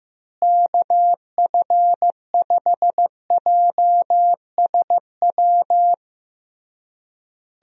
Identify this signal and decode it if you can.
Morse code, heard as a keyed tone, about 15 wpm, 700 Hz. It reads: KF5JSW
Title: USA Call Signs - 6 Characters - Repeated 3x - 15wpm